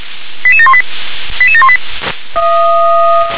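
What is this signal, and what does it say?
Five-tone selective call (ZVEI), heard as quick runs of five tones, tones of about 70 ms. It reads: 88718 88718
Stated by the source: Five-tone ZVEI selective call received on an SDR.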